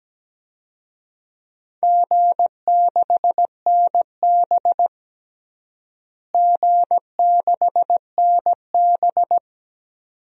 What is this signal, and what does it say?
Morse code, heard as a keyed tone, about 17 wpm, 700 Hz. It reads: G6NB G6NB
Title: DXCC-100 Call Signs - 4 Characters - Repeated 2x - 17wpm